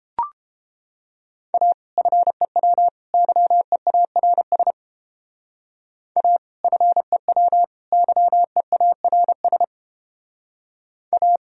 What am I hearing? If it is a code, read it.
Morse code, heard as a keyed tone, about 29 wpm, 700 Hz. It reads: A FEW YEARS A FEW YEARS A